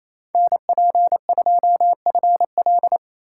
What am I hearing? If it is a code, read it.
Morse code, heard as a keyed tone, about 27 wpm, 700 Hz. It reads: NP2FL
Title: Speed Racing - USA Call Signs 28wpm